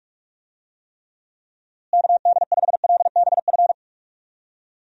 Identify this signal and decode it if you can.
Morse code, heard as a keyed tone, about 41 wpm, 700 Hz. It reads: KD5LBF
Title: USA Call Signs - 6 Characters - Repeated 3x - 45wpm